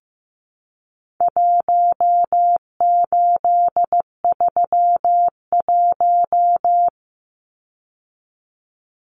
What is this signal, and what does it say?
Morse code, heard as a keyed tone, about 15 wpm, 700 Hz. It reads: 1831
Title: Numbers - Contest Style - 4-Digit 15wpm